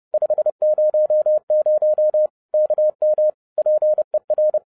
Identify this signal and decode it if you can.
Morse code, heard as a keyed tone, about 30 wpm, 610 Hz. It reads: 500 KM PER
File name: 2011-07-mp3 / n60 BACK FROM A BIG ACCIDENT 30wpm